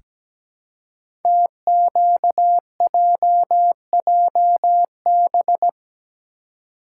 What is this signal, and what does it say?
Morse code, heard as a keyed tone, about 17 wpm, 700 Hz. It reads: TQJJB
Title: Code Groups - 5 Letter:Numbers - 17wpm